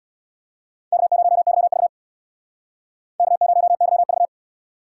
Morse code, heard as a keyed tone, about 60 wpm, 700 Hz. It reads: K0YV K0YV
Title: Speed Racing - USA 1x2 Call Signs 50wpm